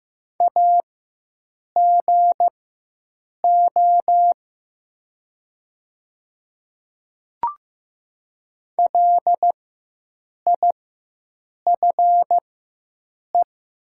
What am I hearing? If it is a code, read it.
Morse code, heard as a keyed tone, about 15 wpm, spaced as for 7 wpm, 700 Hz. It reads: AGO LIFE
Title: 4x Character Spacing - Top 500 Words - 15wpm